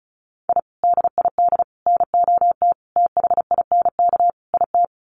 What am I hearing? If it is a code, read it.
Morse code, heard as a keyed tone, about 34 wpm, 700 Hz. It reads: I DID NOT THINK IT
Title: Sets of 5 Words Encore2 35wpm